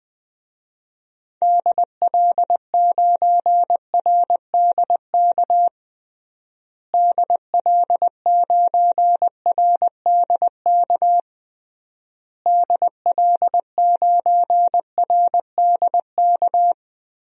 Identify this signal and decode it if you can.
Morse code, heard as a keyed tone, about 20 wpm, 700 Hz. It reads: DL9RDK DL9RDK DL9RDK
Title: DXCC-100 Call Signs - 6 Characters - Repeated 3x - 20wpm